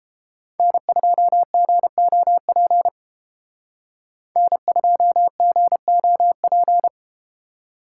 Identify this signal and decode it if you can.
Morse code, heard as a keyed tone, about 30 wpm, 700 Hz. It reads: N2GOP N2GOP